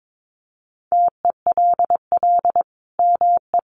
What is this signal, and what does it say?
Morse code, heard as a keyed tone, about 22 wpm, 700 Hz. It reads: TELL ME
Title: Sentences from Top 800 Words 22wpm